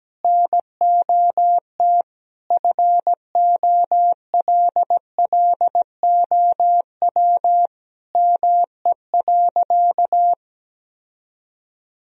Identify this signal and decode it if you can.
Morse code, heard as a keyed tone, about 17 wpm, 700 Hz. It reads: NOT FOLLOW ME.
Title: Sentences from Top 1000 Words - Review Part 1 - 17wpm